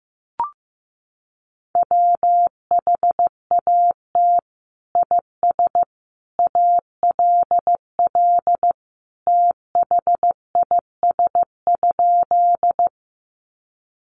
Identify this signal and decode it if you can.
Morse code, heard as a keyed tone, about 15 wpm, 700 Hz. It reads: WHAT IS ALL THIS?
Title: Sentences from Top 100 Words 15wpm